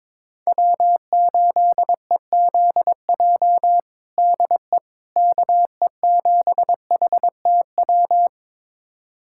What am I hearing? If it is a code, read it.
Morse code, heard as a keyed tone, about 22 wpm, 700 Hz. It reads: W8EZJ DE KE7HTW
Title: USA Callsign DE Callsign 22wpm